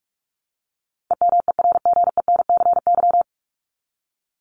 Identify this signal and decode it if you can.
Morse code, heard as a keyed tone, about 44 wpm, 700 Hz. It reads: EMERGENCY
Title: Top 1501-1750 Words 45wpm